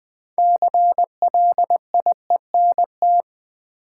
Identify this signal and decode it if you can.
Morse code, heard as a keyed tone, about 20 wpm, 700 Hz. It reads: CLIENT
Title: Six Letter Words - 20wpm